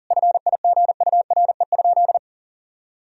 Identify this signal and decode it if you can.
Morse code, heard as a keyed tone, about 37 wpm, 700 Hz. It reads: FIGURE?
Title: Sentences from Top 1000 Words - Review Part 2 - 40wpm